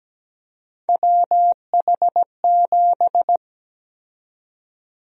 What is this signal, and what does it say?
Morse code, heard as a keyed tone, about 17 wpm, 700 Hz. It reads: WH7
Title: USA Call Sign Prefixes - 17wpm